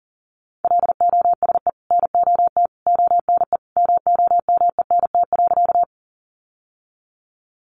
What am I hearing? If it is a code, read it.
Morse code, heard as a keyed tone, about 40 wpm, 700 Hz. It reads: LOSE NOT ONE MOMENT.
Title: Sentences from Top 1100 Words 40wpm